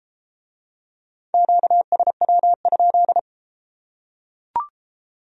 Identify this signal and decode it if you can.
Morse code, heard as a keyed tone, about 31 wpm, 700 Hz. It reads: QSW?